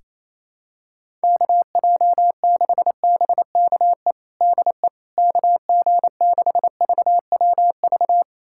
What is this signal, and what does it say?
Morse code, heard as a keyed tone, about 28 wpm, 700 Hz. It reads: KJ6BKE DE KG6VWV
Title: USA Callsign DE Callsign - 6 Characters - 28wpm